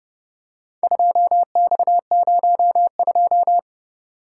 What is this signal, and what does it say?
Morse code, heard as a keyed tone, about 29 wpm, 700 Hz. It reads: 2X02